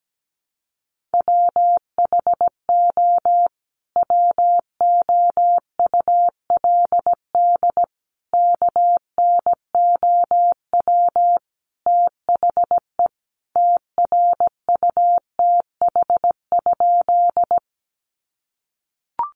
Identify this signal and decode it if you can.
Morse code, heard as a keyed tone, about 17 wpm, 700 Hz. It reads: WHO WOULD KNOW THE TRUTH?
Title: Sentences from Top 800 Words 17wpm